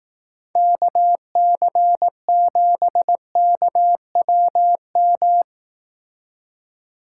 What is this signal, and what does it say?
Morse code, heard as a keyed tone, about 17 wpm, 700 Hz. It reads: KC7KWM
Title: Speed Racing - USA 2x3 Call Signs 15wpm